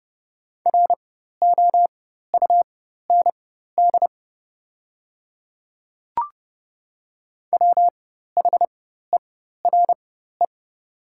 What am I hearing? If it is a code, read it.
Morse code, heard as a keyed tone, about 30 wpm, spaced as for 14 wpm, 700 Hz. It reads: ROUND WHERE
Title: 4x Character Spacing - Top 200 Words - 30wpm